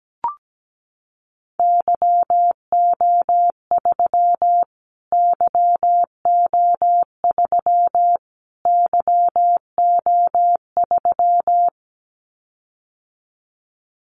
Morse code, heard as a keyed tone, about 17 wpm, 700 Hz. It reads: YO3 YO3 YO3
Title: Callsign Prefixes for Top 25 DXCC - Repeated 3x - 17wpm